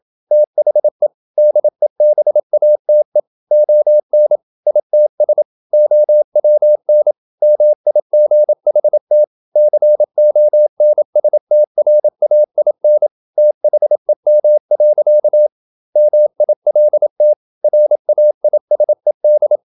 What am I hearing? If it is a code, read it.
Morse code, heard as a keyed tone, about 27 wpm, 600 Hz. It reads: THE DEBATE ON ITS OWN MIGHT CONSTRAIN THEM. MILT RAISED
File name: Never 0027wpm